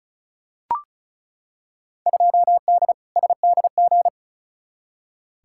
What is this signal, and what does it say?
Morse code, heard as a keyed tone, about 35 wpm, 700 Hz. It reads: 2D SDG
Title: Field Day Exchanges Encore - 35wpm